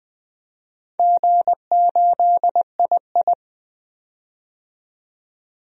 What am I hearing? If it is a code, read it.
Morse code, heard as a keyed tone, about 20 wpm, 700 Hz. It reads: G8II
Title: DXCC-100 Call Signs - 4 Characters - 20wpm